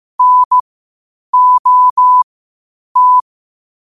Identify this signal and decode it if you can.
Morse code, heard as a keyed tone, about 15 wpm, spaced as for 8 wpm, 1000 Hz. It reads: NOT
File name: qotdfiles / 8wpm